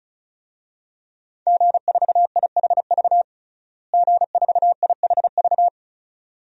Morse code, heard as a keyed tone, about 36 wpm, 700 Hz. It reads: G4IHV G4IHV
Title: DXCC-100 Call Signs - 5 Characters - Repeated 2x - 35wpm